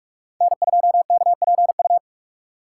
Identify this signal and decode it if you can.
Morse code, heard as a keyed tone, about 43 wpm, 700 Hz. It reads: N2KPU
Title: USA Call Signs - 5 Characters - 6x Character Spacing - 45wpm